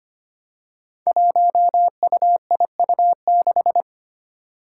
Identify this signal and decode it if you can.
Morse code, heard as a keyed tone, about 25 wpm, 700 Hz. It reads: 1UIU6
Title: Five Challenging Letter-Number Combinations 25wpm